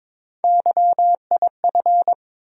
Morse code, heard as a keyed tone, about 22 wpm, 700 Hz. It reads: YIF
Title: USA Call Sign Postfixes - 3 Letters 22wpm